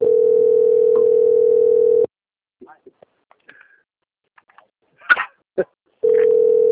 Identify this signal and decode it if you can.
Ringback tone (North America), 440 + 480 Hz, with speech.
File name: out- - - - - .5396